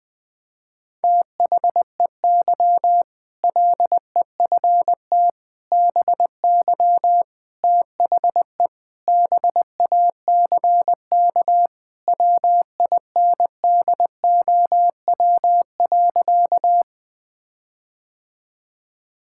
Morse code, heard as a keyed tone, about 21 wpm, 700 Hz. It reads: THEY LEFT BY THE BACK WINDOW.
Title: Sentences from Top 1600 Words 20wpm